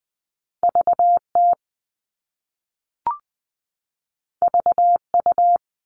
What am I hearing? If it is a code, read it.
Morse code, heard as a keyed tone, about 20 wpm, 700 Hz. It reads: VT VU
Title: All Cut Numbers - 2-Digit 20wpm